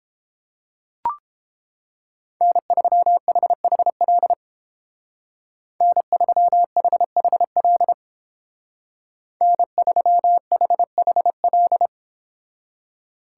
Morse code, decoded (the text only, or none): N3HHL N3HHL N3HHL